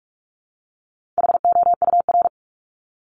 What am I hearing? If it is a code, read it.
Morse code, heard as a keyed tone, about 44 wpm, 700 Hz. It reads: HOUR
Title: ES:S Suffix - Top 100 Words with Base Words 45wpm